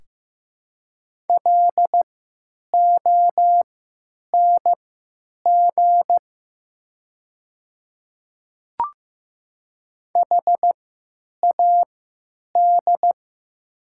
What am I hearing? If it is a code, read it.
Morse code, heard as a keyed tone, about 15 wpm, spaced as for 9 wpm, 700 Hz. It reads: LONG HAD
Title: 3x Character Spacing - Top 100 Words - 15wpm